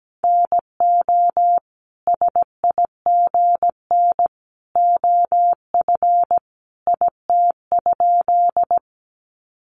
Morse code, heard as a keyed tone, about 17 wpm, 700 Hz. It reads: NO SIGN OF IT?
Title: Sentences from Top 1000 Words 17wpm